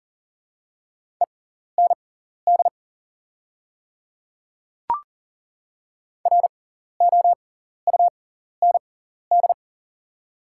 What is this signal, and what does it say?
Morse code, heard as a keyed tone, about 40 wpm, spaced as for 14 wpm, 700 Hz. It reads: END ROUND
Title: 6x Character Spacing - Top 200 Words - 40wpm